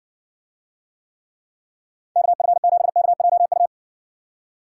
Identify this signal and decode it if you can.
Morse code, heard as a keyed tone, about 67 wpm, 700 Hz. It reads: KF7KJU